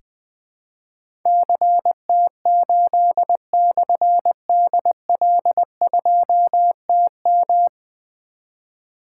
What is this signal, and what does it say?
Morse code, heard as a keyed tone, about 20 wpm, 700 Hz. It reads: CT8/DL2TM
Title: Difficult Call Signs 20wpm